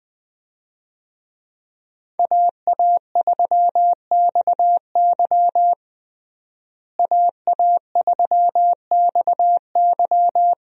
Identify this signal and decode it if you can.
Morse code, heard as a keyed tone, about 20 wpm, 700 Hz. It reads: AA3XY AA3XY
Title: USA Call Signs - 5 Characters - Repeated 2x - 20wpm